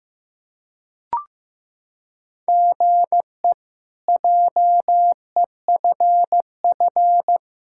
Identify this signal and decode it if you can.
Morse code, heard as a keyed tone, about 15 wpm, 700 Hz. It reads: GE JEFF
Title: K1USN SST QSO Practice - 15wpm